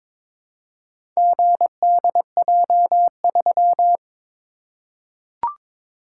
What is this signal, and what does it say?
Morse code, heard as a keyed tone, about 22 wpm, 700 Hz. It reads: GDJ3
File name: practice / 22wpm / Four Letter-Number Combinations - 22wpm